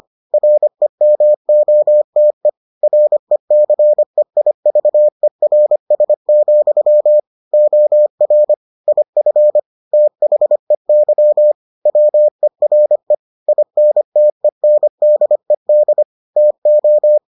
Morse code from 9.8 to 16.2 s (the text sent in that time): THEY WERE INTENDED